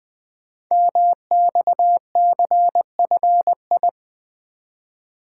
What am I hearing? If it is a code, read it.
Morse code, heard as a keyed tone, about 20 wpm, 700 Hz. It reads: MXCFI
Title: Code Groups - 5 Letters - 20wpm